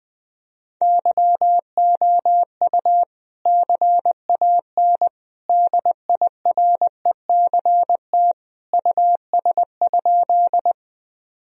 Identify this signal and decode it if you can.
Morse code, heard as a keyed tone, about 20 wpm, 700 Hz. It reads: YOU CAN DIRECT US?
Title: Sentences from Top 1000 Words - Review Part 2 - 20wpm